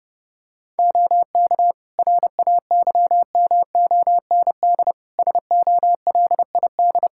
Morse code, heard as a keyed tone, about 32 wpm, 700 Hz. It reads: OK RAYMOND SOLID
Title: Practice QSO - Exchange 3 - More Info - 30wpm